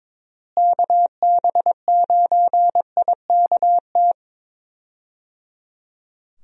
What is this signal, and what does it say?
Morse code, heard as a keyed tone, about 22 wpm, 700 Hz. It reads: KB9IKT
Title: USA Call Signs - 6 Characters - Repeated 3x - 22wpm